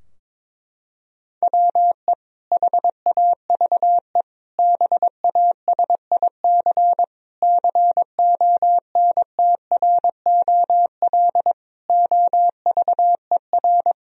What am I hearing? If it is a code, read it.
Morse code, heard as a keyed tone, about 22 wpm, 700 Hz. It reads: WE HAVE BASIC CONTROL OVER
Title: Sentences from Top 1000 Words - Review Part 3 - 22wpm